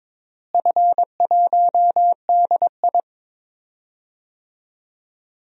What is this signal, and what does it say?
Morse code, heard as a keyed tone, about 22 wpm, 700 Hz. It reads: F1DI
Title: DXCC-100 Call Signs - 4 Characters - Repeated 3x - 22wpm